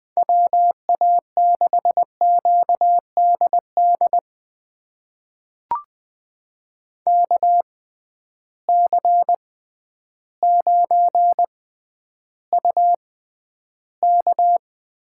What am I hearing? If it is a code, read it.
Morse code, heard as a keyed tone, about 20 wpm, spaced as for 11 wpm, 700 Hz. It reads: WA6QDD KC9UK
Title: USA Call Signs - 6 Characters - 6x Character Spacing - 20wpm